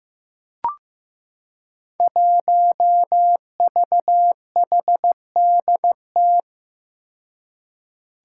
Morse code, heard as a keyed tone, about 15 wpm, 700 Hz. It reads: 1VHDT
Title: Five Challenging Letter-Number Combinations 15wpm